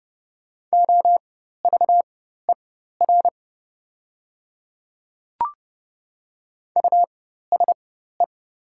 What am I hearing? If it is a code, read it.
Morse code, heard as a keyed tone, about 30 wpm, spaced as for 14 wpm, 700 Hz. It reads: OVER USE
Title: 4x Character Spacing - Top 100 Words - 30wpm